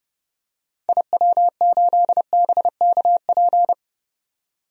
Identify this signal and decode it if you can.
Morse code, heard as a keyed tone, about 30 wpm, 700 Hz. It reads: IW8BKP
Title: DXCC-100 Call Signs - 6 Characters - 30wpm